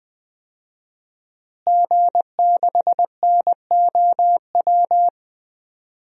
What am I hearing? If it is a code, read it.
Morse code, heard as a keyed tone, about 20 wpm, 700 Hz. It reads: G6NOW
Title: DXCC-100 Call Signs - 5 Characters - Repeated 3x - 20wpm